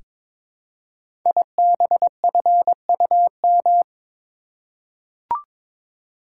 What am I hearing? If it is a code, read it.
Morse code, heard as a keyed tone, about 22 wpm, 700 Hz. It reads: IBFUM
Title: Code Groups - 5 Letter:Numbers - 22wpm